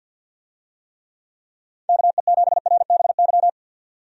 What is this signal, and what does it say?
Morse code, heard as a keyed tone, about 48 wpm, 700 Hz. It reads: KE7RBY
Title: USA Call Signs - 6 Characters - 50wpm